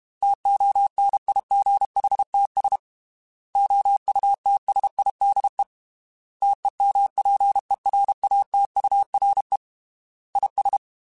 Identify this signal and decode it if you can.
Morse code, heard as a keyed tone, about 32 wpm, 800 Hz. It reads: TONIGHTS OUTSIDE TEMPERATURE IS
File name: Weather Sets / Weather Sets-mp3s / Weather 4 21 @30